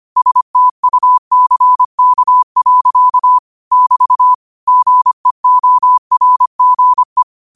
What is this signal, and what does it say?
Morse code, heard as a keyed tone, about 25 wpm, 1000 Hz. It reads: STUCK. <BT> GEORGE